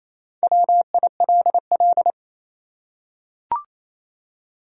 Morse code, heard as a keyed tone, about 28 wpm, 700 Hz. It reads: WILL
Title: USA Common Names - 28wpm